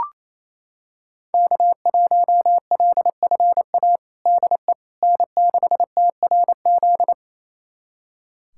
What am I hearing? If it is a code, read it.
Morse code, heard as a keyed tone, about 28 wpm, 700 Hz. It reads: K1LFA DE N6TRZ
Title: USA Callsign DE Callsign - 5 Characters - 28wpm